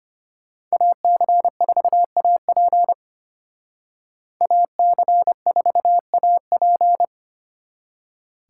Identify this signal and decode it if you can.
Morse code, heard as a keyed tone, about 33 wpm, 700 Hz. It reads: AC4AP AC4AP